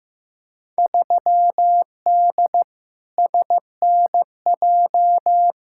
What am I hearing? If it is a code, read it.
Morse code, heard as a keyed tone, about 15 wpm, 700 Hz. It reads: 3D SNJ
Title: Field Day Exchanges Encore - 15wpm